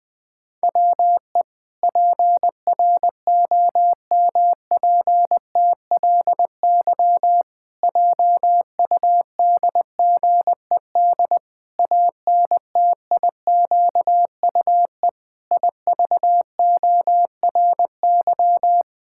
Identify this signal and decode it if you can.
Morse code, heard as a keyed tone, about 20 wpm, 700 Hz. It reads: WE PROMPTLY JUDGED ANTIQUE IVORY
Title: Pangram Challenge v2 - 20wpm